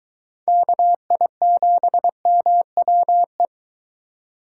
Morse code, heard as a keyed tone, about 23 wpm, 700 Hz. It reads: KI7MWE